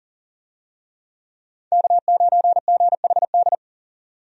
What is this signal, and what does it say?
Morse code, heard as a keyed tone, about 40 wpm, 700 Hz. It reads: K9GHD
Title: USA Call Signs - 5 Characters - 40wpm